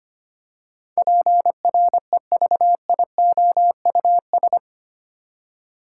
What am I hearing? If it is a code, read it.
Morse code, heard as a keyed tone, about 25 wpm, 700 Hz. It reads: PREVIOUS
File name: 8 Letter Words - First 3 Letters Given - 25wpm